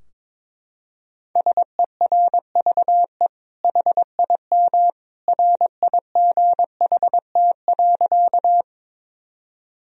Morse code, heard as a keyed tone, about 22 wpm, 700 Hz. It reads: SERVE HIM RIGHT.
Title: Sentences from Top 1000 Words - Review Part 2 - 22wpm